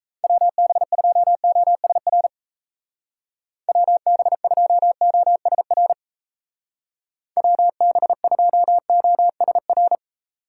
Morse code, heard as a keyed tone, about 37 wpm, 700 Hz. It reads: WB2OSR WB2OSR WB2OSR